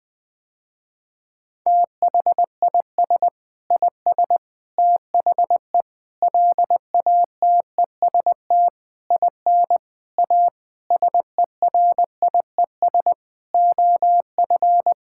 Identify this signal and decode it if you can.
Morse code, heard as a keyed tone, about 20 wpm, 700 Hz. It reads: THIS IS THE LATEST IN A SERIES OF
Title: Sentences with 6-Word Phrases - Spoken Beginning - 20wpm